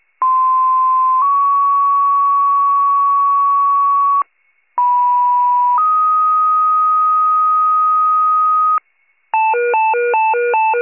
Two-tone page, heard as a one-second tone, then a three-second tone, about 1040 then 1120 Hz, about 980 then 1250 Hz. It is heard from 0.2 to 8.8 s.